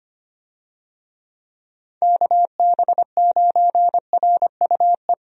Morse code, heard as a keyed tone, about 25 wpm, 700 Hz. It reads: KB9RUE
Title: USA Call Signs - 6 Characters - Repeated 3x - 25wpm